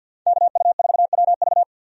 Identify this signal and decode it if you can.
Morse code, heard as a keyed tone, about 50 wpm, 700 Hz. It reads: KU4WV
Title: USA Call Signs - 5 Characters - 50wpm